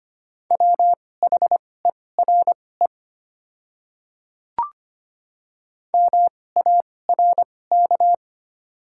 Morse code, heard as a keyed tone, about 25 wpm, spaced as for 18 wpm, 700 Hz. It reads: WHERE MARK
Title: 2x Character Spacing - Top 300 Words - 25wpm